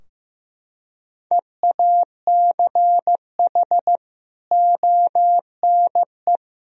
Morse code, heard as a keyed tone, about 15 wpm, 700 Hz. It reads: EACH ONE
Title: Sentences from Top 1000 Words - Review Part 4 - 15wpm